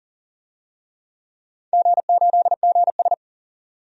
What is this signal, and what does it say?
Morse code, heard as a keyed tone, about 39 wpm, 700 Hz. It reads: G8GS
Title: DXCC-100 Call Signs - 4 Characters - 40wpm